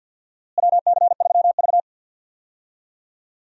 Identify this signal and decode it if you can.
Morse code, heard as a keyed tone, about 51 wpm, 700 Hz. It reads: WC3V
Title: USA Call Signs - 4 Characters - Repeated 3x - 50wpm